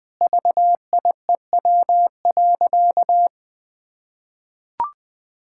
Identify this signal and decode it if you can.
Morse code, heard as a keyed tone, about 19 wpm, 700 Hz. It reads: VIEW.
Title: Sentences from Top 1000 Words - Review Part 6 - 20wpm